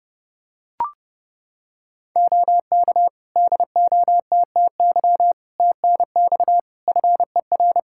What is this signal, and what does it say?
Morse code, heard as a keyed tone, about 30 wpm, 700 Hz. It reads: OK DOTTY TNX FER
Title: Practice QSO - Exchange 4 - Ending - 30wpm